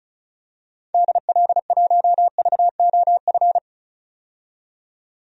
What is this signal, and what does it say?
Morse code, heard as a keyed tone, about 34 wpm, 700 Hz. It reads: DL1VOF
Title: DXCC-100 Call Signs - 6 Characters - 35wpm